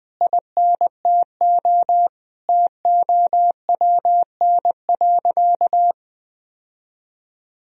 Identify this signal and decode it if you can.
Morse code, heard as a keyed tone, about 20 wpm, 700 Hz. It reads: INTO TOWN.
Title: Sentences from Top 1000 Words - Review Part 1 - 20wpm